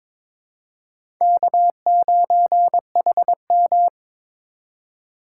Morse code, heard as a keyed tone, about 22 wpm, 700 Hz. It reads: K9HM
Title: USA Call Signs - 4 Characters - 8x Character Spacing - 22wpm